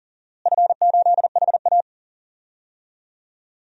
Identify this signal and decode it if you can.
Morse code, heard as a keyed tone, about 40 wpm, 700 Hz. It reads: F8HA
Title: DXCC-100 Call Signs - 4 Characters - Repeated 2x - 40wpm